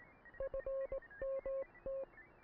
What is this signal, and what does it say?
Morse code, heard as a keyed tone, about 18 wpm, 540 Hz. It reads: FMT